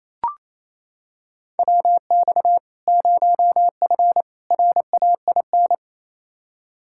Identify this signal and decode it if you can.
Morse code, heard as a keyed tone, about 28 wpm, 700 Hz. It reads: WX 0F RAIN
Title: Practice QSO - Exchange 2 - Info - 28wpm